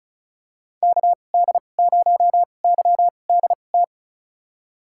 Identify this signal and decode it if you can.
Morse code, heard as a keyed tone, about 34 wpm, spaced as for 25 wpm, 700 Hz. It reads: KD0YDT
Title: USA Call Signs - 6 Characters - 2x Character Spacing - 35wpm